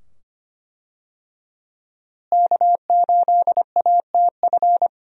Morse code, heard as a keyed tone, about 25 wpm, 700 Hz. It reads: K8ATF
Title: USA Call Signs - 5 Characters - Repeated 2x - 25wpm